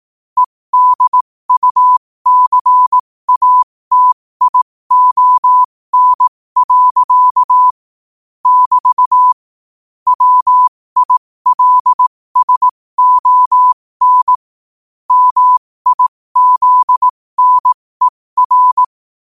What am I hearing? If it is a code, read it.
Morse code, heard as a keyed tone, about 18 wpm, spaced as for 15 wpm, 1000 Hz. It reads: EDUCATION. <BT> WILSON MIZNER
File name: qotdfiles / 15wpm